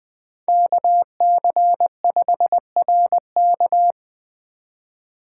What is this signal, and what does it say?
Morse code, heard as a keyed tone, about 20 wpm, 700 Hz. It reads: KC5RK
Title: USA Call Signs - 5 Characters - 4x Character Spacing - 20wpm